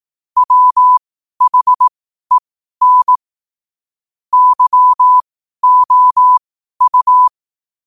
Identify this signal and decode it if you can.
Morse code, heard as a keyed tone, about 18 wpm, spaced as for 12 wpm, 1000 Hz. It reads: WHEN YOU
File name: qotdfiles / 12wpm